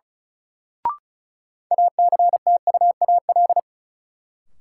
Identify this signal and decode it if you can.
Morse code, heard as a keyed tone, about 34 wpm, 700 Hz. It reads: ACTUAL
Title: Sentences from Top 1000 Words - Review Part 5 - New Word Drill - 35wpm